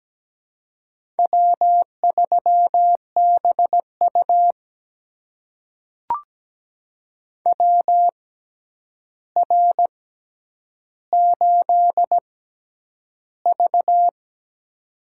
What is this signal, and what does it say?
Morse code, heard as a keyed tone, about 17 wpm, spaced as for 9 wpm, 700 Hz. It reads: W3BU WR8V
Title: USA Call Signs - 4 Characters - 6x Character Spacing - 17wpm